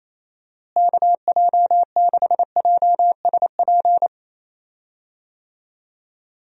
Morse code, heard as a keyed tone, about 28 wpm, 700 Hz. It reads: KJ6JSP
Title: USA Call Signs - 6 Characters - Repeated 2x - 28wpm